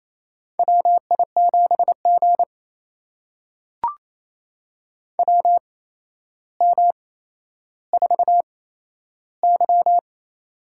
Morse code, heard as a keyed tone, about 28 wpm, spaced as for 12 wpm, 700 Hz. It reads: WI7G WM4Y
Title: USA Call Signs - 4 Characters - 8x Character Spacing - 28wpm